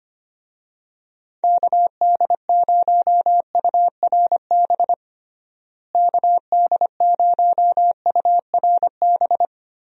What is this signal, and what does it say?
Morse code, heard as a keyed tone, about 25 wpm, 700 Hz. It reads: KD0URB KD0URB
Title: USA Call Signs - 6 Characters - Repeated 3x - 25wpm